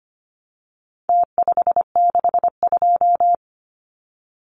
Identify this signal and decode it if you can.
Morse code, heard as a keyed tone, about 25 wpm, 700 Hz. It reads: T562
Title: Numbers - Contest Style - 4-Digit 25wpm